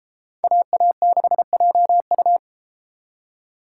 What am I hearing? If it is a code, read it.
Morse code, heard as a keyed tone, about 32 wpm, 700 Hz. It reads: AA6JU